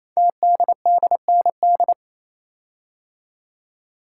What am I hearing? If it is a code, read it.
Morse code, heard as a keyed tone, about 27 wpm, 700 Hz. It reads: TDDND
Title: Code Groups - 5 Letters - 28wpm